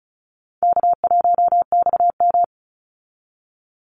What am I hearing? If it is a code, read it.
Morse code, heard as a keyed tone, about 34 wpm, 700 Hz. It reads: K1XM
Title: Easy Call Signs Repeated 3x - 35wpm